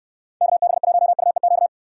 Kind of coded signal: Morse code

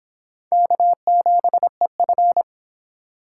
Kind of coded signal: Morse code

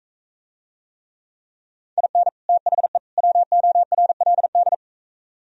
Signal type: Morse code

